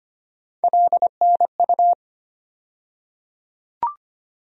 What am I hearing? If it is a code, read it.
Morse code, heard as a keyed tone, about 25 wpm, 700 Hz. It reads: LNU
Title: USA Call Sign Postfixes - 3 Letters 25wpm